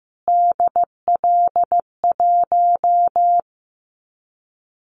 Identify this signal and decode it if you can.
Morse code, heard as a keyed tone, about 15 wpm, 700 Hz. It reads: DL1
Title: Callsign Prefixes for Top 25 DXCC - Repeated 3x - 15wpm